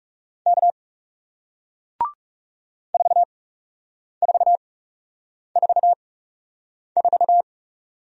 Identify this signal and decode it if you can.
Morse code, heard as a keyed tone, about 41 wpm, spaced as for 8 wpm, 700 Hz. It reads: K 4444